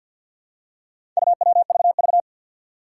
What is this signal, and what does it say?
Morse code, heard as a keyed tone, about 52 wpm, 700 Hz. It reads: UWVV
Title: Four Challenging Letter-Number Combinations 50wpm